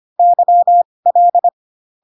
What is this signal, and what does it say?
Morse code, heard as a keyed tone, about 25 wpm, spaced as for 20 wpm, 700 Hz. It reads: YL